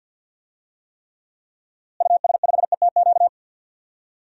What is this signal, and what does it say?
Morse code, heard as a keyed tone, about 47 wpm, 700 Hz. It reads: US5ETQ
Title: DXCC-100 Call Signs - 6 Characters - Repeated 3x - 50wpm